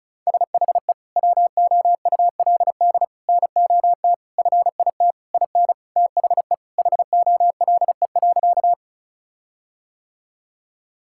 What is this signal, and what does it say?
Morse code, heard as a keyed tone, about 34 wpm, 700 Hz. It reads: SHE WOULD NOT FIT IN THE HOLE.
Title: Sentences from Top 1600 Words 35wpm